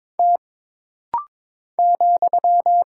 Morse code, heard as a keyed tone, about 22 wpm, 700 Hz. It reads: T ,
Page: This rapid-fire timing is geared to push your ICR (Instant Character Recognition) skills.